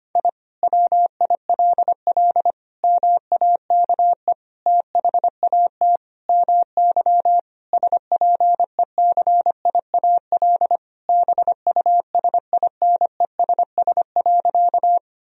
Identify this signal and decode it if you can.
Morse code, heard as a keyed tone, about 25 wpm, 700 Hz. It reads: I WILL MAKE THAT MY SPECIAL BUSINESS.
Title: Sentences from Top 1000 Words - Review Part 3 - 25wpm